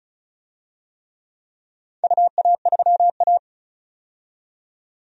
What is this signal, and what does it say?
Morse code, heard as a keyed tone, about 36 wpm, 700 Hz. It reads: UA3A
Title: DXCC-100 Call Signs - 4 Characters - 35wpm